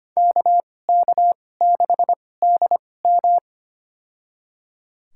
Morse code, heard as a keyed tone, about 25 wpm, spaced as for 18 wpm, 700 Hz. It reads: KK6DM
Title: USA Call Signs - 5 Characters - 2x Character Spacing - 25wpm